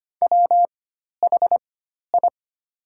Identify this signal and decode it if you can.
Morse code, heard as a keyed tone, about 25 wpm, spaced as for 12 wpm, 700 Hz. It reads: WHI
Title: 4x Character Spacing - Top 500 Words - 25wpm